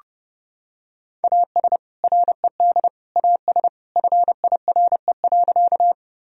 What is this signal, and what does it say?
Morse code, heard as a keyed tone, about 30 wpm, 700 Hz. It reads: AS RED AS FIRE.
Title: Sentences from Top 1000 Words - Review Part 2 - 30wpm